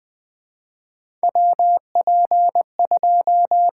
Morse code, heard as a keyed tone, about 20 wpm, 700 Hz. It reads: WP2
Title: USA Call Signs - 6 Characters - 6x Character Spacing - 20wpm